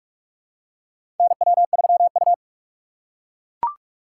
Morse code, heard as a keyed tone, about 44 wpm, 700 Hz. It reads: NW3U
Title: USA Call Signs - 4 Characters - 2x Character Spacing - 45wpm